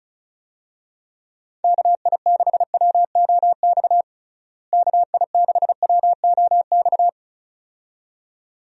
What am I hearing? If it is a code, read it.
Morse code, heard as a keyed tone, about 36 wpm, 700 Hz. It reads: KI6WOX KI6WOX
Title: USA Call Signs - 6 Characters - Repeated 2x - 35wpm